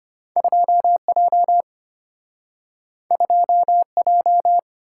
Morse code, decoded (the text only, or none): 2J 2J